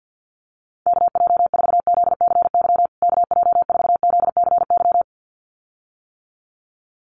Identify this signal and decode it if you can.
Morse code, heard as a keyed tone, about 48 wpm, 700 Hz. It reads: KJ4ZCY KJ4ZCY
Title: USA Call Signs Repeated 2x 50wpm